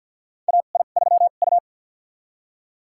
Morse code, heard as a keyed tone, about 52 wpm, spaced as for 37 wpm, 700 Hz. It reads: AI3U
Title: USA Call Signs - 4 Characters - 2x Character Spacing - 50wpm